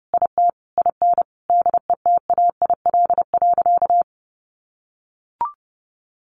Morse code, heard as a keyed tone, about 30 wpm, 700 Hz. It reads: IT IN DETAIL.
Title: Sentences from Top 1500 Words 30wpm